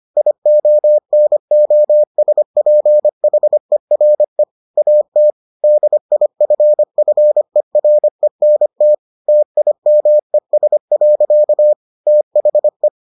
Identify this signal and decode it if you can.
Morse code, heard as a keyed tone, about 25 wpm, 600 Hz. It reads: IONOSPHERE AT DIFFERENT TIMES. THE